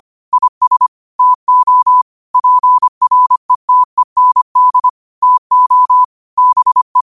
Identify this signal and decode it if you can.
Morse code, heard as a keyed tone, about 25 wpm, 1000 Hz. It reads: IS TO PRETEND TO BE
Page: New quotes every day in morse code at 25 Words per minute.